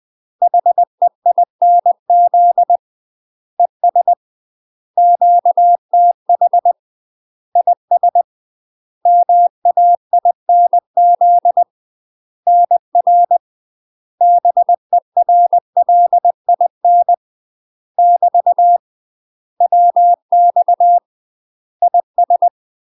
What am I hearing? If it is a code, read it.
Morse code, heard as a keyed tone, about 20 wpm, 700 Hz. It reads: HEINZ ES QTH IS MAINZ NR BERLIN <BT> WX IS
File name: UK MASTER QSOs TRAINING / 20WPM / UK MASTER QSOs 0046